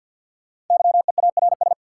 Morse code, heard as a keyed tone, about 51 wpm, 700 Hz. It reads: YEARS
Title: Sets of 2 Words - New Word Drill - 50wpm